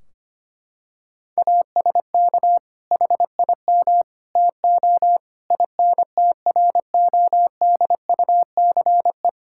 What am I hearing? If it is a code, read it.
Morse code, heard as a keyed tone, about 25 wpm, 700 Hz. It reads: ASK HIM TO INTRODUCE
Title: Sentences from Top 1600 Words 25wpm